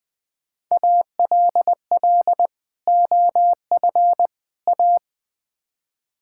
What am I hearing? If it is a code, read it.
Morse code, heard as a keyed tone, about 20 wpm, 700 Hz. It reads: ALL OF A
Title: Build A Phrase - 4-Word Phrases - 20wpm